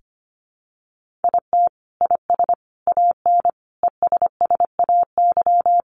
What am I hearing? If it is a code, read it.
Morse code, heard as a keyed tone, about 25 wpm, 700 Hz. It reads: IT IS AN ESSAY
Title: Sentences from Top 1500 Words 25wpm